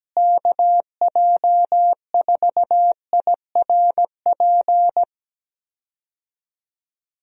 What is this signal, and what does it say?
Morse code, heard as a keyed tone, about 17 wpm, 700 Hz. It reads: KJ4IRP
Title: USA Call Signs - 6 Characters - Repeated 3x - 17wpm